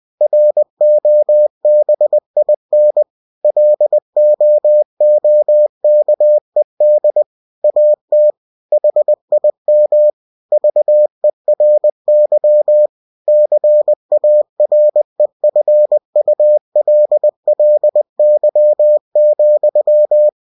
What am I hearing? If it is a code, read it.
Morse code, heard as a keyed tone, about 20 wpm, 600 Hz. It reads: ROBIN LOOKED AT HIM VERY CAREFULLY,